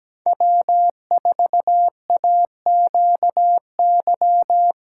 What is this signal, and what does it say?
Morse code, heard as a keyed tone, about 17 wpm, 700 Hz. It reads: W4AQY
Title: USA Callsign DE Callsign - 5 Characters - 17wpm